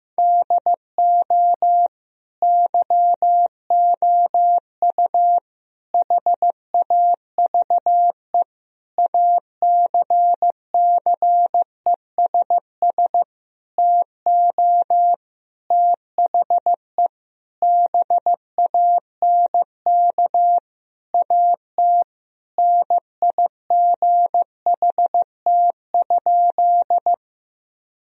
Morse code, heard as a keyed tone, about 15 wpm, 700 Hz. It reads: DO YOU HAVE ACCESS TO THE BANK AT NIGHT?
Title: Sentences from Top 1000 Words - Review Part 4 - 15wpm